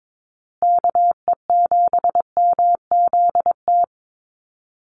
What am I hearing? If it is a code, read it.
Morse code, heard as a keyed tone, about 22 wpm, 700 Hz. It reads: KE7MZT
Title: USA Call Signs Repeated 2x 22wpm